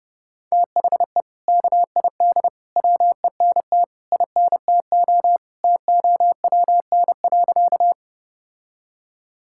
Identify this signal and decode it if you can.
Morse code, heard as a keyed tone, about 30 wpm, 700 Hz. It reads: THE KID WENT INTO TOWN.
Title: Sentences from Top 1600 Words 30wpm